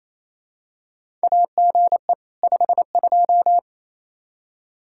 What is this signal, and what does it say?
Morse code, heard as a keyed tone, about 28 wpm, 700 Hz. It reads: AGE 52
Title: Practice QSO - Exchange 3 - More Info - 28wpm